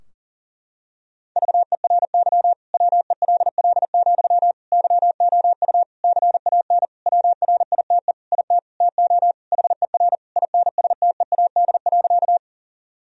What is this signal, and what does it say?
Morse code, heard as a keyed tone, about 40 wpm, 700 Hz. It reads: VERY WELL, YOU CAN WRITE IT TO HER INSTEAD.
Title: Sentences from Top 1000 Words - Review Part 4 - 40wpm